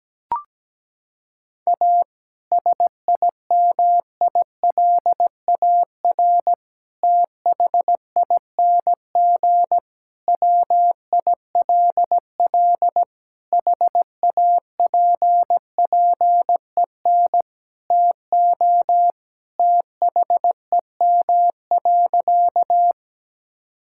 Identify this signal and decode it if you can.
Morse code, heard as a keyed tone, about 17 wpm, 700 Hz. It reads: A SIMILAR THING WILL HAPPEN TO THEM.
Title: Sentences from Top 1000 Words - Review Part 5 - 17wpm